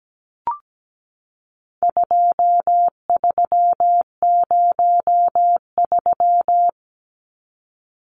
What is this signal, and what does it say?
Morse code, heard as a keyed tone, about 17 wpm, 700 Hz. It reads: 2303
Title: Numbers - 4-Digit 17wpm